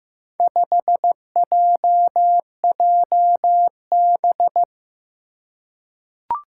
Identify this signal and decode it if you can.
Morse code, heard as a keyed tone, about 15 wpm, 700 Hz. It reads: 5JJB
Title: Four Challenging Letter-Number Combinations 15wpm